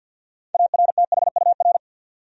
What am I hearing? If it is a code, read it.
Morse code, heard as a keyed tone, about 52 wpm, 700 Hz. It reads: ARTHUR
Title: USA Common Names - 50wpm